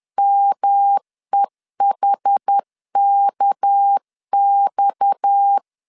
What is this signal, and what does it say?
Morse code, heard as a keyed tone, about 10 wpm, 800 Hz. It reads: MEHKX
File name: LEZIONI / 6 PKX0 / 6 PKX 10wpm 2426